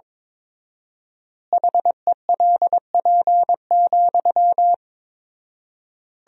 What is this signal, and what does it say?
Morse code, heard as a keyed tone, about 22 wpm, 700 Hz. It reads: HELP,
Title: 4x Word-Spacing - Sentences from Top 300 Words - 22wpm